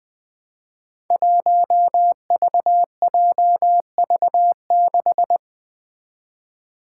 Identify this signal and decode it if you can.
Morse code, heard as a keyed tone, about 20 wpm, 700 Hz. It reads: 1VJV6
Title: Five Challenging Letter-Number Combinations 20wpm